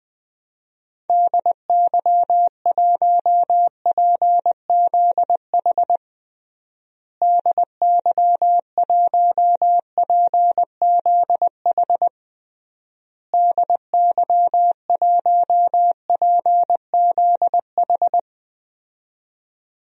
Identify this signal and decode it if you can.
Morse code, heard as a keyed tone, about 20 wpm, 700 Hz. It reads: DY1PZH DY1PZH DY1PZH
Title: DXCC-100 Call Signs - 6 Characters - Repeated 3x - 20wpm